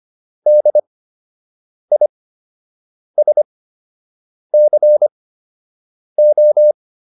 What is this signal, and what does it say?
Morse code, heard as a keyed tone, about 25 wpm, spaced as for 7 wpm, 600 Hz. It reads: DISCO